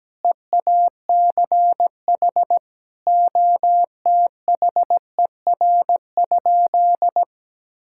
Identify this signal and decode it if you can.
Morse code, heard as a keyed tone, about 17 wpm, 700 Hz. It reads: EACH OTHER?
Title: Sentences from Top 1000 Words - Review Part 6 - 17wpm